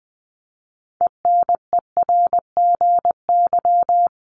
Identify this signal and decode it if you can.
Morse code, heard as a keyed tone, about 20 wpm, 700 Hz. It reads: ENERGY
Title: Top 751-1000 Words 20wpm